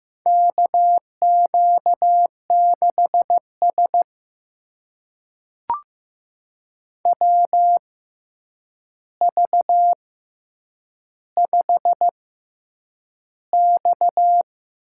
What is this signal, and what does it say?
Morse code, heard as a keyed tone, about 15 wpm, spaced as for 8 wpm, 700 Hz. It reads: KQ6S WV5X
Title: USA Call Signs - 4 Characters - 6x Character Spacing - 15wpm